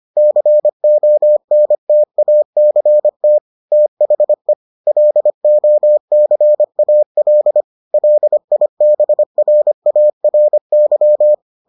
Morse code, heard as a keyed tone, about 25 wpm, 600 Hz. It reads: CONTACT THE LOCAL LIBRARY